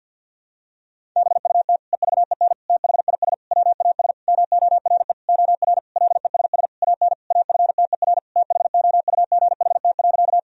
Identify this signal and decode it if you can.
Morse code, heard as a keyed tone, about 50 wpm, 700 Hz. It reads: BUT EVEN THIS WAS MORE OR LESS AN AFTER THOUGHT.
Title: Sentences from Top 1000 Words - Review Part 2 - 50wpm